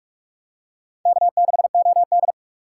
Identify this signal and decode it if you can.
Morse code, heard as a keyed tone, about 48 wpm, 690 Hz. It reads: K6OD
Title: USA Call Signs - 4 Characters - 3x Character Spacing - 45wpm